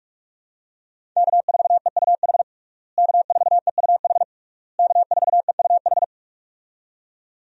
Morse code, heard as a keyed tone, about 46 wpm, 700 Hz. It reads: K4EUH K4EUH K4EUH
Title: USA Call Signs - 5 Characters - Repeated 3x - 45wpm